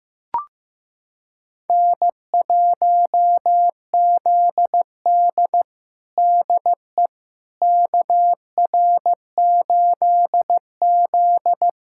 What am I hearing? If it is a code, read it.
Morse code, heard as a keyed tone, about 15 wpm, 700 Hz. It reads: N1ZD DE KR8Z
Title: USA Callsign DE Callsign - 4 Characters - 15wpm